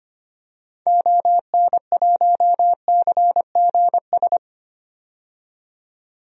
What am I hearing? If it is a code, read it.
Morse code, heard as a keyed tone, about 25 wpm, 700 Hz. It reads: ON1CGS
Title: DXCC-100 Call Signs - 6 Characters - 25wpm